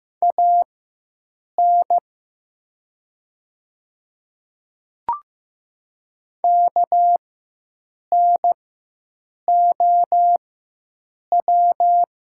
Morse code, heard as a keyed tone, about 15 wpm, spaced as for 7 wpm, 700 Hz. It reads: AN KNOW
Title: 4x Character Spacing - Top 100 Words - 15wpm